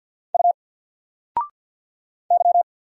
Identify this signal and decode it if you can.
Morse code, heard as a keyed tone, about 50 wpm, 700 Hz. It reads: U Y
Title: Single Letter-Number 50wpm